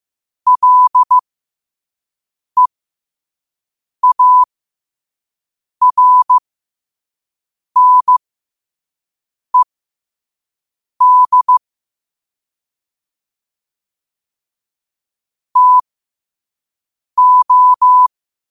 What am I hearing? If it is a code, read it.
Morse code, heard as a keyed tone, about 15 wpm, spaced as for 5 wpm, 1000 Hz. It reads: LEARNED TO
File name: qotdfiles / 5wpm